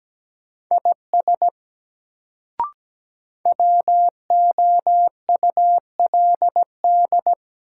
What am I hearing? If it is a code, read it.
Morse code, heard as a keyed tone, about 17 wpm, 700 Hz. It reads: IS WOULD
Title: Sentences from Top 100 Words - New Word Drill - 17wpm